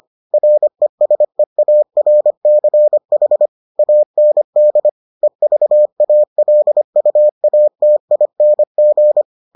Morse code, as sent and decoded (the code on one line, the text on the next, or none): .-. ... .- .-. -.-. .... / .- -. -.. / . ...- .- .-.. ..- .- - .. -. --.
RESEARCH AND EVALUATING